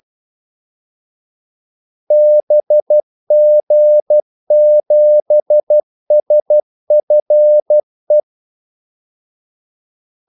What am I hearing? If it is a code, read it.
Morse code, heard as a keyed tone, about 12 wpm, 600 Hz. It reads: BG7SFE
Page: 2 ekstra ordmellemrum mellem kaldesignalerne